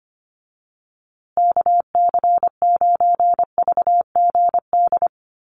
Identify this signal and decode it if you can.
Morse code, heard as a keyed tone, about 25 wpm, 700 Hz. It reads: KC9VGD
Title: USA Call Signs Repeated 3x 25wpm